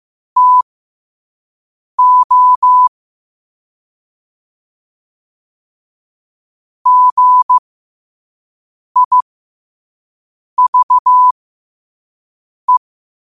Morse code, heard as a keyed tone, about 15 wpm, spaced as for 5 wpm, 1000 Hz. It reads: TO GIVE
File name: qotdfiles / 5wpm